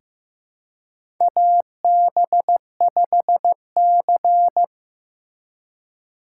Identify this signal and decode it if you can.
Morse code, heard as a keyed tone, about 15 wpm, 700 Hz. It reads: AB5C
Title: K1USN SST QSO Practice - 15wpm